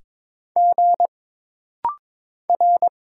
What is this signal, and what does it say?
Morse code, heard as a keyed tone, about 22 wpm, 700 Hz. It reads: G R
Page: This rapid-fire timing is geared to push your ICR (Instant Character Recognition) skills.